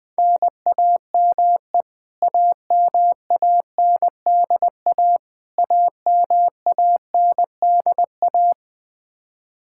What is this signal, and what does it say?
Morse code, heard as a keyed tone, about 21 wpm, 700 Hz. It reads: NAME AMANDA AMANDA
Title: K1USN SST QSO Practice 20wpm